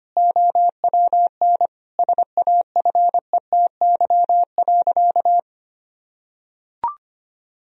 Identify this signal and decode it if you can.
Morse code, heard as a keyed tone, about 25 wpm, 700 Hz. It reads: OWN SAFETY.
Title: Sentences from Top 1000 Words - Review Part 4 - 25wpm